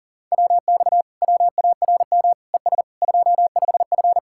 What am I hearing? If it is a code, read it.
Morse code, heard as a keyed tone, about 49 wpm, 700 Hz. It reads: WX WARM ES 25F